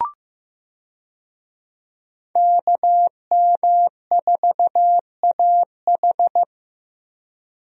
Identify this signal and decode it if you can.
Morse code, heard as a keyed tone, about 15 wpm, 700 Hz. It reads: KM4AH
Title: USA Call Signs - 5 Characters - Repeated 2x - 15wpm